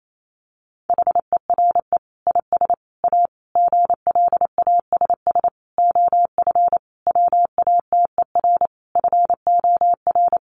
Morse code, heard as a keyed tone, about 28 wpm, 700 Hz. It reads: HERE IS A GLASS OF WATER FOR
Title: Sentences from Top 1300 Words 28wpm